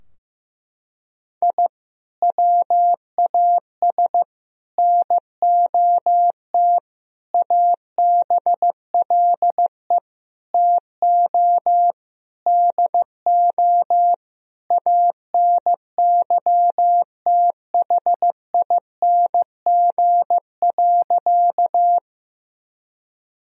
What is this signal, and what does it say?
Morse code, heard as a keyed tone, about 15 wpm, 700 Hz. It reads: I WAS NOT ABLE TO DO ANYTHING.
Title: Sentences from Top 1000 Words - Review Part 2 - 15wpm